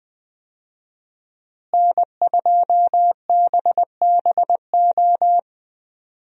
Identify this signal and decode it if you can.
Morse code, heard as a keyed tone, about 21 wpm, 700 Hz. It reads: N2BBO
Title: USA Call Signs - 5 Characters - Repeated 2x - 20wpm